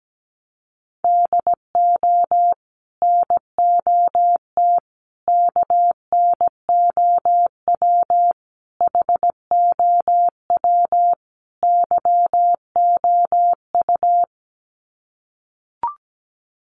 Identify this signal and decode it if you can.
Morse code, heard as a keyed tone, about 17 wpm, 700 Hz. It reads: DO NOT KNOW HOW YOU
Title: Sets of 5 Words Encore 17wpm